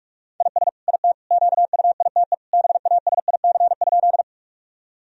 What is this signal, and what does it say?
Morse code, heard as a keyed tone, about 46 wpm, 700 Hz. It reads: IS IT QUITE BASIC?
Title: Sentences from Top 1000 Words - Review Part 3 - 45wpm